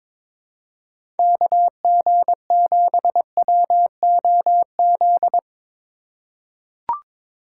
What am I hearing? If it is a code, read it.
Morse code, heard as a keyed tone, about 22 wpm, 700 Hz. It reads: KG7WOZ
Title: USA Call Signs - 2x Character Spacing - 22wpm